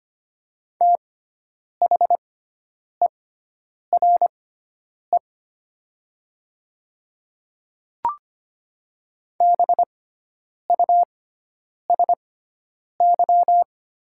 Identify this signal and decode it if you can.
Morse code, heard as a keyed tone, about 25 wpm, spaced as for 9 wpm, 700 Hz. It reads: THERE BUSY
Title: 6x Character Spacing - Top 500 Words - 25wpm